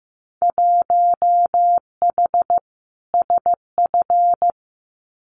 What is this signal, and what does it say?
Morse code, heard as a keyed tone, about 15 wpm, 700 Hz. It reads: 1H SF
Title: Winter Field Day Exchanges 15wpm